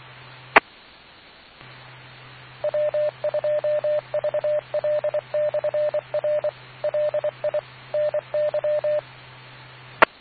Morse code, heard as a keyed tone, about 24 wpm, 600 Hz, with radio noise. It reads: E W2VL/R LI NY E